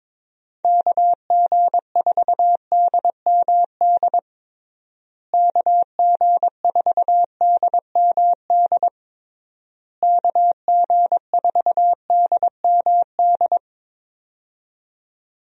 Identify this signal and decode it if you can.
Morse code, heard as a keyed tone, about 22 wpm, 700 Hz. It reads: KG4DMD KG4DMD KG4DMD
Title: USA Call Signs - 6 Characters - Repeated 3x - 22wpm